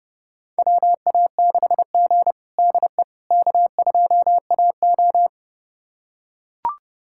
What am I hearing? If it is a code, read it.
Morse code, heard as a keyed tone, about 32 wpm, 700 Hz. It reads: WA6G DE K2AO
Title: USA Callsign DE Callsign - 4 Characters - 30wpm